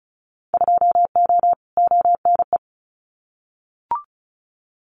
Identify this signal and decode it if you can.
Morse code, heard as a keyed tone, about 34 wpm, 700 Hz. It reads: 2O ONE
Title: Winter Field Day Exchanges Encore 35wpm